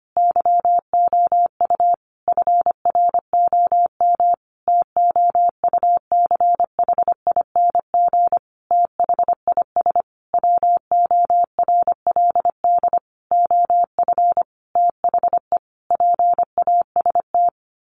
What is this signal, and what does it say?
Morse code, heard as a keyed tone, about 25 wpm, 700 Hz. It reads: YOU FROM TOUCHING THIS WORLD OF THE PAST
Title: A Sound Of Thunder 25wpm